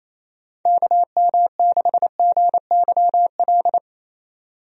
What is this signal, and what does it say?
Morse code, heard as a keyed tone, about 28 wpm, 700 Hz. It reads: KM6GYL
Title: USA Call Signs - 6 Characters - 28wpm